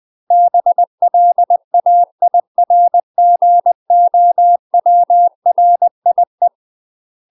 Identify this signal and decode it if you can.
Morse code, heard as a keyed tone, about 20 wpm, 700 Hz. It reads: BLAIRGOWRIE
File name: UK MASTER QSOs TRAINING / 20WPM / UK MASTER QSOs 0018